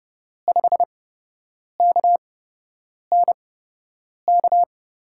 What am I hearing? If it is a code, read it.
Morse code, heard as a keyed tone, about 32 wpm, spaced as for 8 wpm, 700 Hz. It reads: 5KNK
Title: USA Call Signs - 6 Characters - 8x Character Spacing - 30wpm